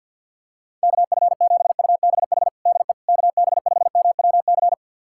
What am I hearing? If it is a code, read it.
Morse code, heard as a keyed tone, about 50 wpm, 700 Hz. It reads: KF7UBH DE KB5MWC
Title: USA Callsign DE Callsign - 6 Characters - 50wpm